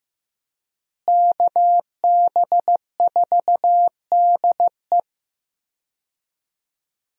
Morse code, heard as a keyed tone, about 15 wpm, 700 Hz. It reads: KB4DE
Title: USA Call Signs - 5 Characters - Repeated 3x - 15wpm